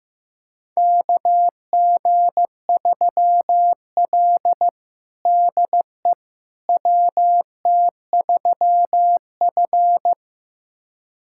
Morse code, heard as a keyed tone, about 15 wpm, 700 Hz. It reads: KG3L DE WT3F
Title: USA Callsign DE Callsign - 4 Characters - 15wpm